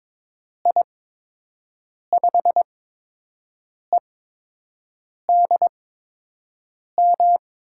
Morse code, heard as a keyed tone, about 22 wpm, spaced as for 6 wpm, 700 Hz. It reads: I5EDM
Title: USA Call Signs - 6 Characters - 8x Character Spacing - 22wpm